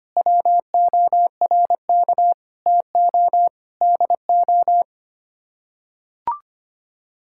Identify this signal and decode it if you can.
Morse code, heard as a keyed tone, about 25 wpm, 700 Hz. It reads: WORK TO DO
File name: Sets of 9 Words - 25wpm